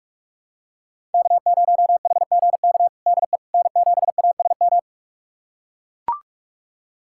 Morse code, heard as a keyed tone, about 42 wpm, 700 Hz. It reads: K0HGK DE N7ASM
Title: USA Callsign DE Callsign - 5 Characters - 45wpm